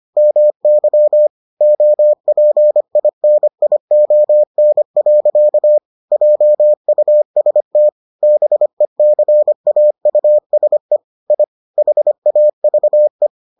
Morse code, heard as a keyed tone, about 25 wpm, 600 Hz. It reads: MY OPINION. JUST BECAUSE I HAVE